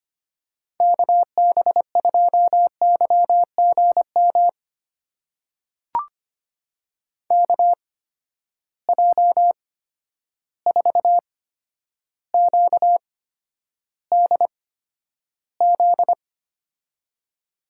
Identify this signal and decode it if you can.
Morse code, heard as a keyed tone, about 25 wpm, spaced as for 11 wpm, 700 Hz. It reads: KB2YGM KJ4QDZ
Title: USA Call Signs - 6 Characters - 8x Character Spacing - 25wpm